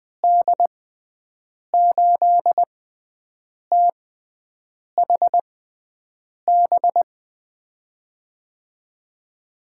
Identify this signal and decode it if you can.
Morse code, heard as a keyed tone, about 20 wpm, spaced as for 7 wpm, 700 Hz. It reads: D8THB
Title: USA Call Signs - 6 Characters - 6x Character Spacing - 20wpm